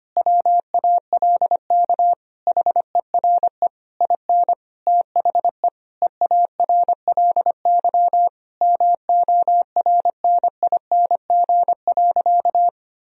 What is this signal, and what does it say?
Morse code, heard as a keyed tone, about 25 wpm, 700 Hz. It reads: WALK HERE IN THE EARLY MORNING.
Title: Sentences from Top 1000 Words - Review Part 3 - 25wpm